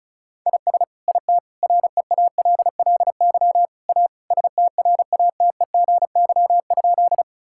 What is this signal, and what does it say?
Morse code, heard as a keyed tone, about 36 wpm, 700 Hz. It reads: IS IT REALLY A STRATEGY?
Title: Sentences from Top 1000 Words - Review Part 5 - 35wpm